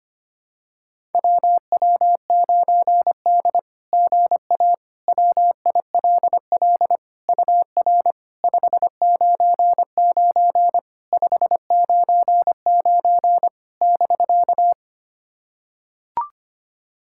Morse code, heard as a keyed tone, about 25 wpm, 690 Hz. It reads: WW9D GA WILL UR 599 599 <BK>
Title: SOTA QSO Practice 25wpm